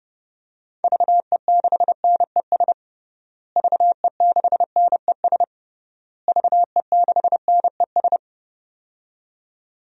Morse code, heard as a keyed tone, about 32 wpm, 700 Hz. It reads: VE6NES VE6NES VE6NES
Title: DXCC-100 Call Signs - 6 Characters - Repeated 3x - 30wpm